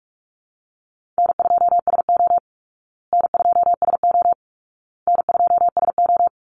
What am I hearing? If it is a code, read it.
Morse code, heard as a keyed tone, about 44 wpm, 700 Hz. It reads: N2SO N2SO N2SO
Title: Easy Call Signs Repeated 3x - 45wpm